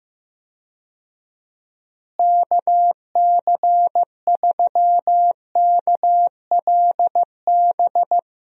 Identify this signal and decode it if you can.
Morse code, heard as a keyed tone, about 15 wpm, 700 Hz. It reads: KC3KLB
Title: USA Call Signs - 6 Characters - Repeated 3x - 15wpm